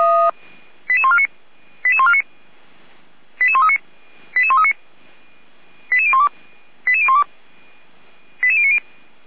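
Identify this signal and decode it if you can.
Five-tone selective call (ZVEI), heard as quick runs of five tones, tones of about 70 ms. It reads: 88149 88149 88139 88139 88012 88012 88090
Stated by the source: Five-tone ZVEI selective call received by SDR.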